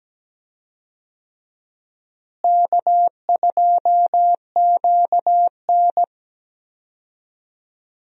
Morse code, heard as a keyed tone, about 17 wpm, 700 Hz. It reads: K2QN
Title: USA Call Signs - 4 Characters - Repeated 2x - 17wpm